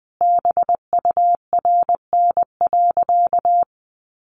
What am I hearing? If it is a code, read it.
Morse code, heard as a keyed tone, about 20 wpm, 700 Hz. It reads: BURN.
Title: Sentences from Top 1300 Words 20wpm